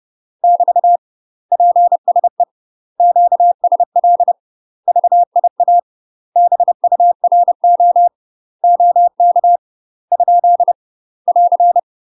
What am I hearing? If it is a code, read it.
Morse code, heard as a keyed tone, about 30 wpm, 700 Hz. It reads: <BT> PSE QSL VIA BURO OK ? <AR>
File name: UK MASTER QSOs TRAINING/30WPM